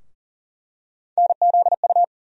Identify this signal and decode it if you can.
Morse code, heard as a keyed tone, about 40 wpm, 700 Hz. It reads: NZU
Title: USA Call Sign Postfixes - 3 Letters 40wpm